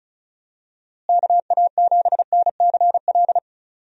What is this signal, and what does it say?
Morse code, heard as a keyed tone, about 36 wpm, 700 Hz. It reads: KA7NCL
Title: USA Call Signs - 6 Characters - Repeated 3x - 35wpm